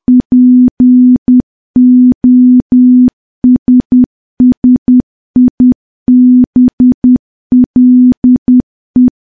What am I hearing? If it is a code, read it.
Morse code, heard as a keyed tone, about 10 wpm, 250 Hz. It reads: POSSIBLE
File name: morse / morse-training-10WPM